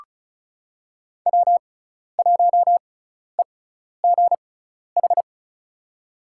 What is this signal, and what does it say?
Morse code, heard as a keyed tone, about 36 wpm, spaced as for 12 wpm, 700 Hz. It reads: W1EGH
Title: USA Call Signs - 6x Character Spacing - 35wpm